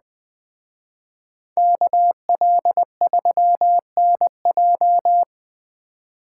Morse code, heard as a keyed tone, about 20 wpm, 700 Hz. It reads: KL3NJ
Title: Speed Racing - USA 2x2 Call Signs 17wpm